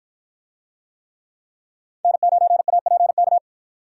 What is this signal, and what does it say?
Morse code, heard as a keyed tone, about 53 wpm, 700 Hz. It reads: N9APK